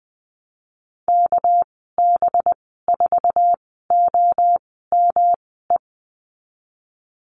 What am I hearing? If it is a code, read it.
Morse code, heard as a keyed tone, about 20 wpm, spaced as for 15 wpm, 700 Hz. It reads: KB4OME
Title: 200 Lesson 32g - Letter k - Call Signs 20wpm